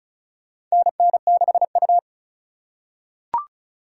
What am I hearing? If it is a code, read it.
Morse code, heard as a keyed tone, about 36 wpm, 700 Hz. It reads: NN6U
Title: USA Call Signs - 4 Characters - 4x Character Spacing - 35wpm